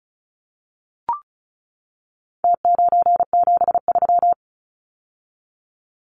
Morse code, heard as a keyed tone, about 34 wpm, 700 Hz. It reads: T973
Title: Numbers - Contest Style - 4-Digit 35wpm